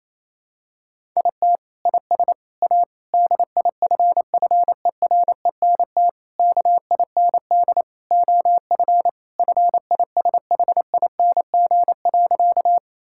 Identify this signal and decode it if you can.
Morse code, heard as a keyed tone, about 27 wpm, 700 Hz. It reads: IT IS A DIFFERENT KIND OF FISHING.
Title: Sentences from Top 1000 Words - Review Part 5 - 28wpm